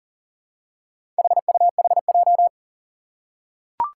Morse code, heard as a keyed tone, about 42 wpm, 700 Hz. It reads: HUHJ
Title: Four Challenging Letter-Number Combinations 40wpm